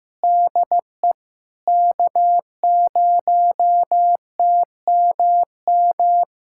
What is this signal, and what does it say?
Morse code, heard as a keyed tone, about 15 wpm, 700 Hz. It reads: DE K0TMM
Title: USA Callsign DE Callsign 15wpm